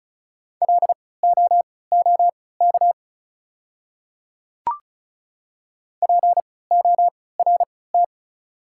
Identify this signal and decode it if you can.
Morse code, heard as a keyed tone, about 36 wpm, spaced as for 20 wpm, 690 Hz. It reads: LOOK PORT
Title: 3x Character Spacing - Top 200 Words - 35wpm